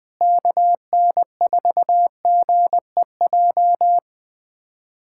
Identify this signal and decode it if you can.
Morse code, heard as a keyed tone, about 20 wpm, 700 Hz. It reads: KN4GEJ
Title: USA Callsign DE Callsign 20wpm